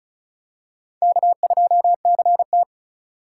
Morse code, heard as a keyed tone, about 36 wpm, 700 Hz. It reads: K2CT
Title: USA Call Signs - 4x Character Spacing - 35wpm